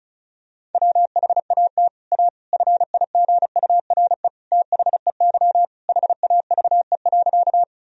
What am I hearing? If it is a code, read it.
Morse code, heard as a keyed tone, about 36 wpm, 700 Hz. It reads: WHAT A FIGURE THEY HAVE.
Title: Sentences from Top 1000 Words - Review Part 2 - 35wpm